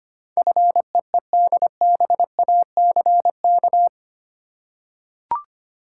Morse code, heard as a keyed tone, about 25 wpm, 700 Hz. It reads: FEEDBACK
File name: 8 Letter Words - First 3 Letters Given - 25wpm